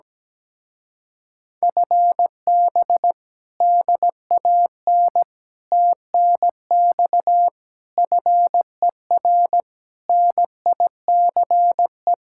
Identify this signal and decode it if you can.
Morse code, heard as a keyed tone, about 17 wpm, 700 Hz. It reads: FB DAN TNX FER NICE
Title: Speed-Racing Common QSO Phrases - 17wpm